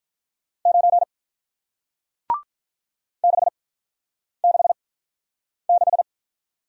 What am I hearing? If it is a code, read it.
Morse code, heard as a keyed tone, about 47 wpm, 700 Hz. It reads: 9 6 6 6